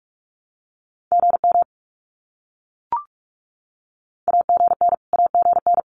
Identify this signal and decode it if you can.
Morse code, heard as a keyed tone, about 44 wpm, 700 Hz. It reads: GM AGN AGN
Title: Abbreviations Repeated 3x - 45wpm